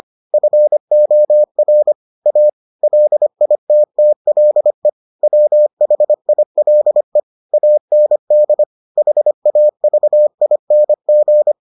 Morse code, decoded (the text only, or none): FOR A LITTLE WHILE AND HAVING